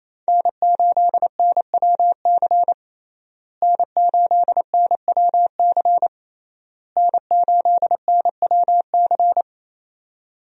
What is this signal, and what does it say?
Morse code, heard as a keyed tone, about 28 wpm, 700 Hz. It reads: N8NWC N8NWC N8NWC
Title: USA Call Signs - 5 Characters - Repeated 3x - 28wpm